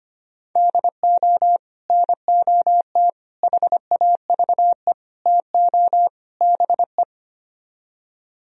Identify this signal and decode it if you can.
Morse code, heard as a keyed tone, about 30 wpm, 700 Hz. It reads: DO NOT HAVE TO BE
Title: Speed-Racing - Sets of 5 Words 25wpm